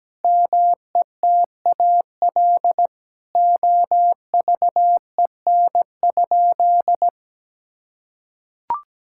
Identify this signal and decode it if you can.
Morse code, heard as a keyed tone, about 17 wpm, 700 Hz. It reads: METAL OVEN?
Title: Sentences from Top 1000 Words - Review Part 6 - 17wpm